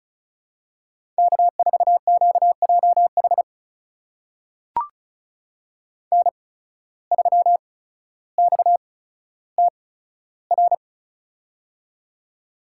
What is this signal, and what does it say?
Morse code, heard as a keyed tone, about 36 wpm, spaced as for 15 wpm, 700 Hz. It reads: K4QJH N3XTR
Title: USA Call Signs - 5 Characters - 8x Character Spacing - 35wpm